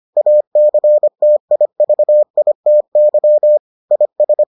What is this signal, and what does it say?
Morse code, heard as a keyed tone, about 25 wpm, 600 Hz. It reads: ACTIVITY IS